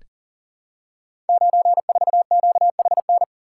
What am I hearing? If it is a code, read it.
Morse code, heard as a keyed tone, about 38 wpm, 700 Hz. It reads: 94QHN
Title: Code Groups - 5 Letter:Numbers - 40wpm